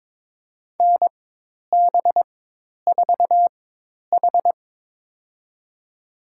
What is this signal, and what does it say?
Morse code, heard as a keyed tone, about 23 wpm, spaced as for 10 wpm, 700 Hz. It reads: NB4H
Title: USA Call Signs - 4 Characters - 4x Character Spacing - 22wpm